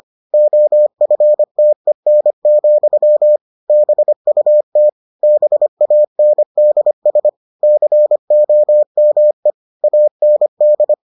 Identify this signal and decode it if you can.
Morse code, heard as a keyed tone, about 25 wpm, 600 Hz. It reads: OFTEN, BUT BANDS COME AND